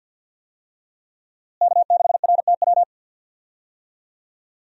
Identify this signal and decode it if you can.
Morse code, heard as a keyed tone, about 51 wpm, 700 Hz. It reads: K6RTW
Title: USA Call Signs - 5 Characters - Repeated 2x - 50wpm